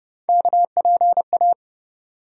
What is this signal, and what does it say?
Morse code, heard as a keyed tone, about 31 wpm, 700 Hz. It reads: KPA
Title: USA Call Sign Postfixes - 3 Letters 30wpm